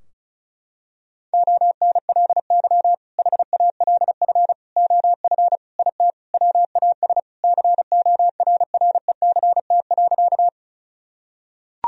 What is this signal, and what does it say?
Morse code, heard as a keyed tone, about 36 wpm, 700 Hz. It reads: ONLY HALF OF IT WAS CORRECT.
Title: Sentences from Top 1000 Words - Review Part 2 - 35wpm